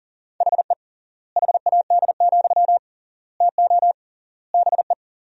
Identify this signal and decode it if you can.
Morse code, heard as a keyed tone, about 40 wpm, 700 Hz. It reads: HE HAD, TO BE
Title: 3x Word-Spacing - Sentences from Top 500 Words - 40wpm